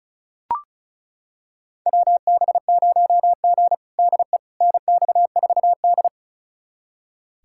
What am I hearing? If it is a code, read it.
Morse code, heard as a keyed tone, about 36 wpm, 700 Hz. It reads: WB0G DE NX4D
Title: USA Callsign DE Callsign - 4 Characters - 35wpm